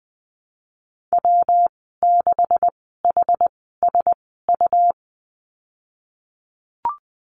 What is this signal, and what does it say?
Morse code, heard as a keyed tone, about 20 wpm, spaced as for 15 wpm, 700 Hz. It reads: W6HSU
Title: 161 Lesson 26g - Letter g - Call Signs 20wpm